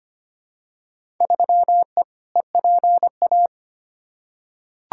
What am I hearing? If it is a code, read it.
Morse code, heard as a keyed tone, about 25 wpm, 700 Hz. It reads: 3E EPA
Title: Field Day Exchanges Encore - 25wpm